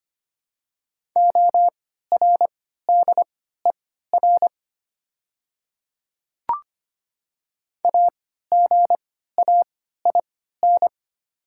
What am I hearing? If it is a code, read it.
Morse code, heard as a keyed tone, about 25 wpm, spaced as for 14 wpm, 700 Hz. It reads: ORDER AGAIN
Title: 3x Character Spacing - Top 500 Words - 25wpm